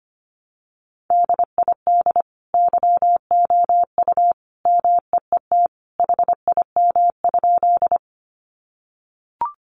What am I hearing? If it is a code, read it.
Morse code, heard as a keyed tone, about 25 wpm, 700 Hz. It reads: DID YOU MEET HIM?
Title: Sentences from Top 1100 Words 25wpm